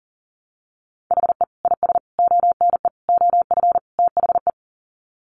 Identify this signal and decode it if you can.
Morse code, heard as a keyed tone, about 40 wpm, 700 Hz. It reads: HE IS ONE OF THE
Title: Sets of 5 Words Encore 40wpm